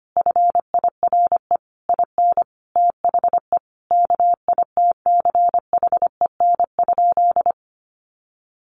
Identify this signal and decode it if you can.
Morse code, heard as a keyed tone, about 25 wpm, 700 Hz. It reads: FIRE IN THE KITCHEN?
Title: Sentences from Top 1400 Words 25wpm